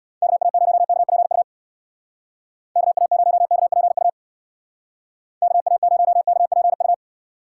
Morse code, heard as a keyed tone, about 62 wpm, 700 Hz. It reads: KA0KPU KA0KPU KA0KPU
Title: Speed Racing - USA 2x3 Call Signs 50wpm